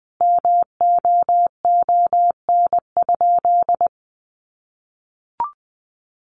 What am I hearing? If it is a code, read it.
Morse code, heard as a keyed tone, about 20 wpm, 700 Hz. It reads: MOON?
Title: Sentences from Top 1200 Words 20wpm